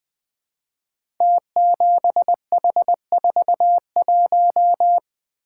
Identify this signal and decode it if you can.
Morse code, heard as a keyed tone, about 21 wpm, 700 Hz. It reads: T7H41
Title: Code Groups - 5 Letter:Numbers - 20wpm